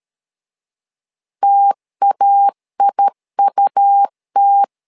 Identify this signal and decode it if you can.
Morse code, heard as a keyed tone, about 13 wpm, 800 Hz. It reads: TAIUT
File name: LEZIONI / 2 UHDN / 2 UHDN 12wpm 1456